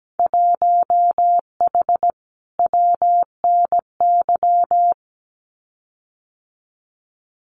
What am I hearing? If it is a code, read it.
Morse code, heard as a keyed tone, about 17 wpm, 700 Hz. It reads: 1H WNY
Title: Winter Field Day Exchanges 17wpm